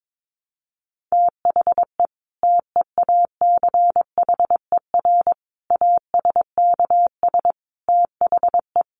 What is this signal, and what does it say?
Morse code, heard as a keyed tone, about 22 wpm, 700 Hz. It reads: THE TEACHER ASKS THE
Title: Additional Easy Phrases 22wpm